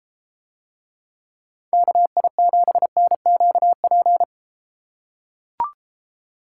Morse code, heard as a keyed tone, about 33 wpm, 700 Hz. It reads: KI7NQP